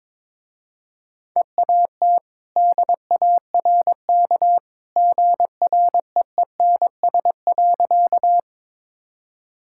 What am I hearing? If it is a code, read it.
Morse code, heard as a keyed tone, about 22 wpm, 700 Hz. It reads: EAT DARK GREENS.
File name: practice / 22wpm / Easy Sentences with Hints - 22wpm